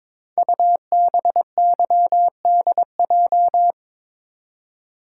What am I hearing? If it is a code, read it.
Morse code, heard as a keyed tone, about 22 wpm, 700 Hz. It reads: UBYDJ
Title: Code Groups - 5 Letters - 22wpm